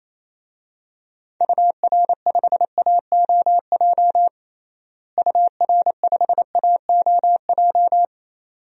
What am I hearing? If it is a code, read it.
Morse code, heard as a keyed tone, about 28 wpm, 700 Hz. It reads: UR5AOJ UR5AOJ
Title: DXCC-100 Call Signs - 6 Characters - Repeated 2x - 28wpm